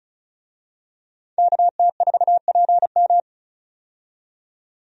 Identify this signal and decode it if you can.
Morse code, heard as a keyed tone, about 34 wpm, 700 Hz. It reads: KT4PM
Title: USA Call Signs - 5 Characters - 35wpm